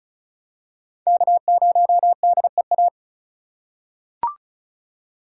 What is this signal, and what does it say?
Morse code, heard as a keyed tone, about 36 wpm, 700 Hz. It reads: K0DEA
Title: USA Call Signs - 5 Characters - 3x Character Spacing - 35wpm